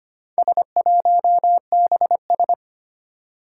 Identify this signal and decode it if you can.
Morse code, heard as a keyed tone, about 25 wpm, 700 Hz. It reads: S1BS
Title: Four Challenging Letter-Number Combinations 25wpm